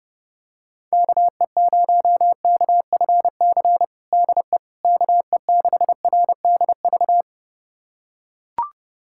Morse code, heard as a keyed tone, about 32 wpm, 700 Hz. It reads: KE0KFC DE KE6RDV
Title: USA Callsign DE Callsign - 6 Characters - 30wpm